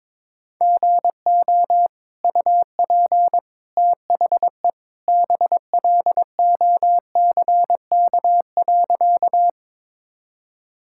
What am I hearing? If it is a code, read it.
Morse code, heard as a keyed tone, about 22 wpm, 700 Hz. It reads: GO UP THE BLOCK.
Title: Sentences from Top 1600 Words 22wpm